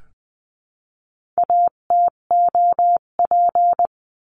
Morse code, heard as a keyed tone, about 20 wpm, 700 Hz. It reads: ATOP
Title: 258 Lesson 42c - Words 16-20wpm